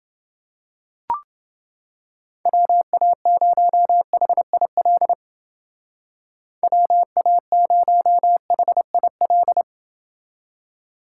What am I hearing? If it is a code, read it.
Morse code, heard as a keyed tone, about 29 wpm, 700 Hz. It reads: WA0HIL WA0HIL